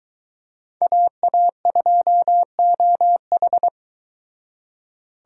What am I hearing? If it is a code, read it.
Morse code, heard as a keyed tone, about 22 wpm, 700 Hz. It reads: AA2OH